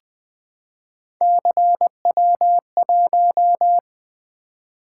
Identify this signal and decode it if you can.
Morse code, heard as a keyed tone, about 20 wpm, 700 Hz. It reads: CW1
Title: Three Letter-Number Combinations - 20wpm